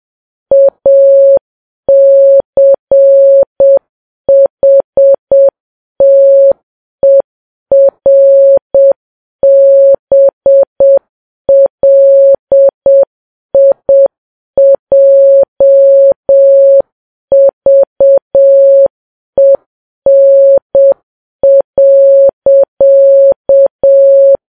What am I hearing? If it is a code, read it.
Morse code, heard as a keyed tone, about 7 wpm, 550 Hz. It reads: ACHTERBLIJVEN.
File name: Files Web tls / Morse on9fv / morse7kps